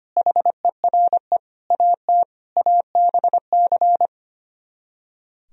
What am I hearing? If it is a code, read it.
Morse code, heard as a keyed tone, about 24 wpm, 700 Hz. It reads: HERE AT ABC
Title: Speed-Racing - Sets of 5 Words 25wpm